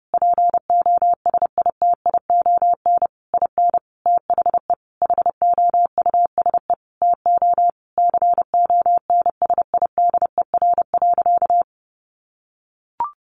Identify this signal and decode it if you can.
Morse code, heard as a keyed tone, about 30 wpm, 700 Hz. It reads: POSITION IN THE HOUSE TO CONSIDER.
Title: Sentences from Top 900 Words 30wpm